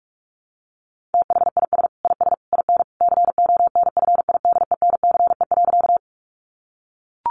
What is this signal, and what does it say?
Morse code, heard as a keyed tone, about 44 wpm, 700 Hz. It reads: THIS IS IN CONFIDENCE.
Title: Sentences from Top 1500 Words 45wpm